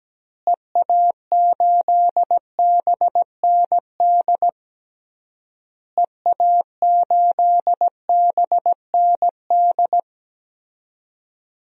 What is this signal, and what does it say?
Morse code, heard as a keyed tone, about 17 wpm, 700 Hz. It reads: EA8BND EA8BND
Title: DXCC-100 Call Signs - 6 Characters - Repeated 3x - 17wpm